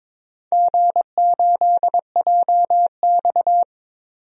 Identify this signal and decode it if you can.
Morse code, heard as a keyed tone, about 22 wpm, 700 Hz. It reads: G8JX
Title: DXCC-100 Call Signs - 4 Characters - Repeated 2x - 22wpm